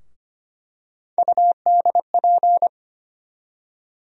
Morse code, heard as a keyed tone, about 25 wpm, 700 Hz. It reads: UDP
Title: USA Call Sign Postfixes - 3 Letters 25wpm